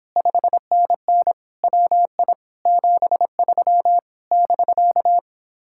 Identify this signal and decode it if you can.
Morse code, heard as a keyed tone, about 29 wpm, 700 Hz. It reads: 5NN WI 73 <BK>